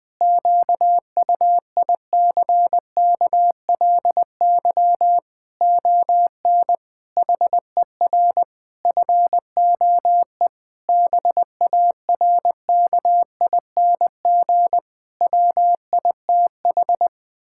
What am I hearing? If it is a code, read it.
Morse code, heard as a keyed tone, about 21 wpm, 700 Hz. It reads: QUICKLY ON HER FOE BARKING WITH
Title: Pangram Challenge v2 - 20wpm